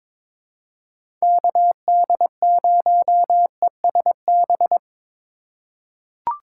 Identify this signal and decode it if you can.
Morse code, heard as a keyed tone, about 21 wpm, 700 Hz. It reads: KD0ESB
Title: USA Call Signs - 6 Characters - 4x Character Spacing - 22wpm